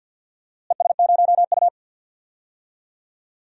Morse code, heard as a keyed tone, about 50 wpm, 700 Hz. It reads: ES0U
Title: DXCC-100 Call Signs - 4 Characters - Repeated 3x - 50wpm